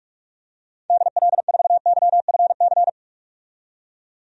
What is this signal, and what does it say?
Morse code, heard as a keyed tone, about 44 wpm, 700 Hz. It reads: DL4YFC
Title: DXCC-100 Call Signs - 6 Characters - 45wpm